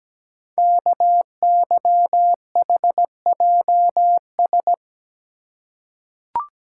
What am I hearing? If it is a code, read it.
Morse code, heard as a keyed tone, about 17 wpm, 700 Hz. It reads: KYHJS
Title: Code Groups - 5 Letters - 17wpm